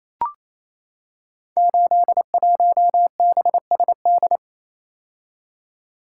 Morse code, heard as a keyed tone, about 28 wpm, 700 Hz. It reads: 81BSD
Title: Code Groups - 5 Letter:Numbers - 28wpm